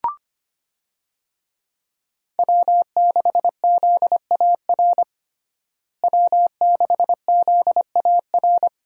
Morse code, heard as a keyed tone, about 25 wpm, 700 Hz. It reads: W6ZAR W6ZAR
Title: USA Call Signs - 5 Characters - Repeated 2x - 25wpm